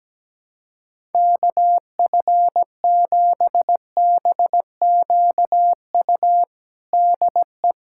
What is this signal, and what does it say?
Morse code, heard as a keyed tone, about 17 wpm, 700 Hz. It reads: KF7BQU DE
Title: USA Callsign DE Callsign 17wpm